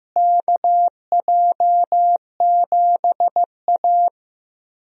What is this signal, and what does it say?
Morse code, heard as a keyed tone, about 15 wpm, 700 Hz. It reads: KJ7A
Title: USA Call Signs - 4 Characters - Repeated 3x - 15wpm